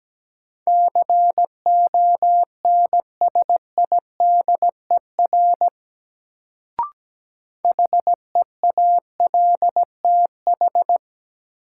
Morse code, heard as a keyed tone, about 17 wpm, 700 Hz. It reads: CONSIDER HEALTH
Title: Sentences from Top 600 Words - New Word Drill - 17wpm